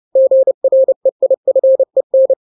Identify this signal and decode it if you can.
Morse code, heard as a keyed tone, about 30 wpm, 530 Hz. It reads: GREIFEN
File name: daily / 29wpm